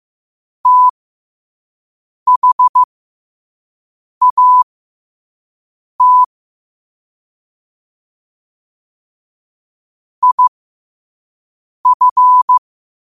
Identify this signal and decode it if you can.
Morse code, heard as a keyed tone, about 15 wpm, spaced as for 5 wpm, 1000 Hz. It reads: THAT IF
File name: qotdfiles / 5wpm